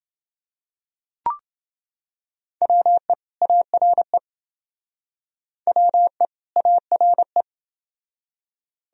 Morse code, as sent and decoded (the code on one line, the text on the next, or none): .-- . / .- .-. / .-- . / .- .-. .
WE ARE WE ARE